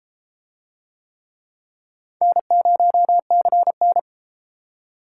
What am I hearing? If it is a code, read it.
Morse code, heard as a keyed tone, about 31 wpm, 700 Hz. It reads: N0CN